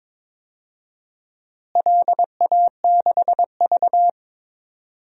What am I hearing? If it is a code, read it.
Morse code, heard as a keyed tone, about 22 wpm, 700 Hz. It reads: LA6V
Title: DXCC-100 Call Signs - 4 Characters - Repeated 2x - 22wpm